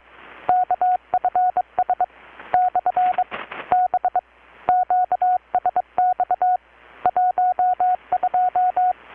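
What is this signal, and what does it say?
Morse code, heard as a keyed tone, about 22 wpm, 720 Hz, with radio noise. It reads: KFS / B QSX 12
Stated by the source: Morse from KPH's Night of Nights XX maritime radiotelegraphy event, received in Chiapas, Mexico about 3,800 km away on a Kenwood R-600 with a grounded 10 m long-wire antenna.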